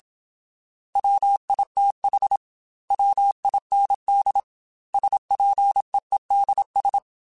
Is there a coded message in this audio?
Yes: Morse code from 1.0 to 7.0 s, heard as a keyed tone, about 27 wpm, 800 Hz. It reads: WITH WIND SPEEDS